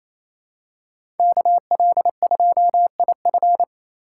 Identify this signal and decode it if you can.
Morse code, heard as a keyed tone, about 28 wpm, 700 Hz. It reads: KL2IF
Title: USA Call Signs - 4x Character Spacing - 28wpm